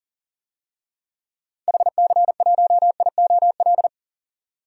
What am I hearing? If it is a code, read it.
Morse code, heard as a keyed tone, about 38 wpm, 700 Hz. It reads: HC1IOL
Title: DXCC-100 Call Signs - 6 Characters - 40wpm